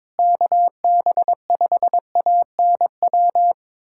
Morse code, heard as a keyed tone, about 22 wpm, 700 Hz. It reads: KB5ANW
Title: USA Callsign DE Callsign - 6 Characters - 22wpm